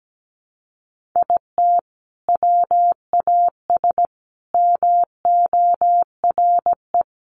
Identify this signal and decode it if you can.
Morse code, heard as a keyed tone, about 17 wpm, 700 Hz. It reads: IT WAS MORE
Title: Sentences from Top 1400 Words 17wpm